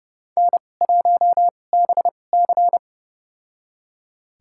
Morse code, heard as a keyed tone, about 30 wpm, spaced as for 22 wpm, 700 Hz. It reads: N1BC
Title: USA Call Signs - 4 Characters - 2x Character Spacing - 30wpm